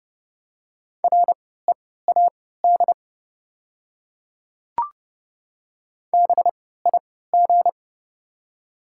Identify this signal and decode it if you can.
Morse code, heard as a keyed tone, about 30 wpm, spaced as for 17 wpm, 700 Hz. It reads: READ BIG
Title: 3x Character Spacing - Top 300 Words - 30wpm